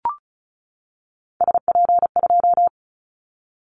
Morse code, heard as a keyed tone, about 36 wpm, 700 Hz. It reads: SP2
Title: Callsign Prefixes for Top 25 DXCC - 35wpm